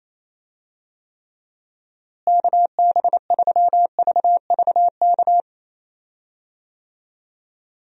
Morse code, heard as a keyed tone, about 28 wpm, 700 Hz. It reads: KB3VVK
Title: USA Call Signs - 6 Characters - Repeated 3x - 28wpm